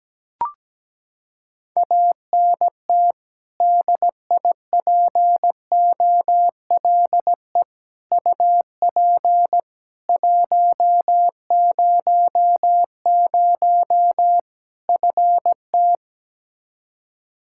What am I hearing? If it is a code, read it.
Morse code, heard as a keyed tone, about 17 wpm, 700 Hz. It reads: ANT DIPOLE UP 100 FT
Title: Practice QSO - Exchange 2 - Info - 17wpm